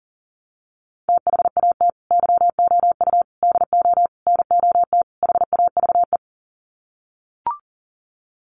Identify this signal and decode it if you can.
Morse code, heard as a keyed tone, about 40 wpm, 700 Hz. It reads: THAT YOU DO NOT HAVE
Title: Sets of 5 Words Encore 40wpm